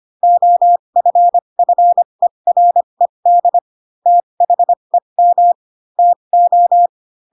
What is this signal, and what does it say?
Morse code, heard as a keyed tone, about 25 wpm, spaced as for 22 wpm, 700 Hz. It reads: OFFERED THEM TO
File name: Pinocchio / Pinocchio ch7 18 25 ik1hge